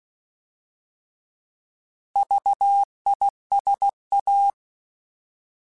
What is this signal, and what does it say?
Morse code, heard as a keyed tone, about 16 wpm, 800 Hz. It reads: VISA